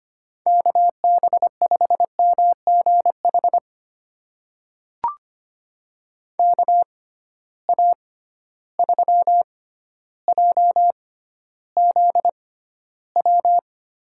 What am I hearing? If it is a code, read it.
Morse code, heard as a keyed tone, about 25 wpm, spaced as for 13 wpm, 700 Hz. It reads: KB5MGH KA3JZW
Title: USA Call Signs - 6 Characters - 6x Character Spacing - 25wpm